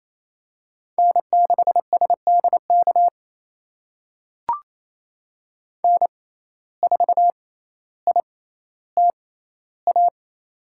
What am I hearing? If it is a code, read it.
Morse code, heard as a keyed tone, about 28 wpm, spaced as for 14 wpm, 700 Hz. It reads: N6SDK N4ITA
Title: USA Call Signs - 5 Characters - 6x Character Spacing - 28wpm